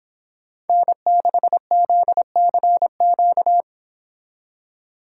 Morse code, heard as a keyed tone, about 26 wpm, 700 Hz. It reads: N6ZCQ